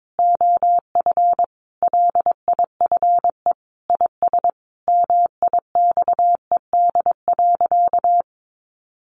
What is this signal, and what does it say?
Morse code, heard as a keyed tone, about 22 wpm, 700 Hz. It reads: OF LIFE IS MIXED.
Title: Sentences from Top 1500 Words 22wpm